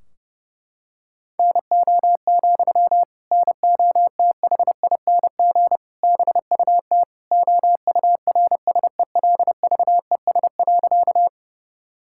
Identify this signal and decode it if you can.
Morse code, heard as a keyed tone, about 30 wpm, 700 Hz. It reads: NO, NOTHING BUT OURSELVES.
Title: Sentences from Top 1600 Words 30wpm